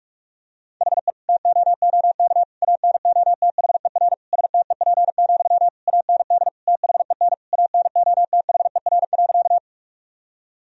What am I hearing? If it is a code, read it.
Morse code, heard as a keyed tone, about 41 wpm, 700 Hz. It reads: HE TOOK ANOTHER STEP, AND THEN ANOTHER.
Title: Sentences from Top 1000 Words - Review Part 2 - 45wpm